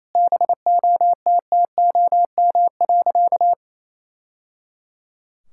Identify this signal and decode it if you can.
Morse code, heard as a keyed tone, about 28 wpm, 700 Hz. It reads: BOTTOM.
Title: Sentences from Top 1000 Words - Review Part 6 - 28wpm